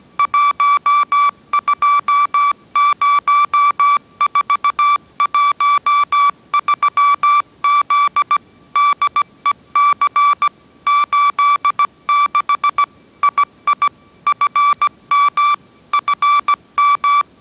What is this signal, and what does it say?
Morse code, heard as a keyed tone, about 17 wpm, 1160 Hz. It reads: 120413Z DEC 86 II FM FM
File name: 18wpm